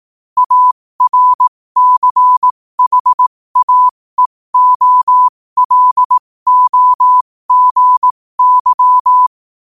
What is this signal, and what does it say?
Morse code, heard as a keyed tone, about 18 wpm, spaced as for 15 wpm, 1000 Hz. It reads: ARCHAEOLOGY
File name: qotdfiles / 15wpm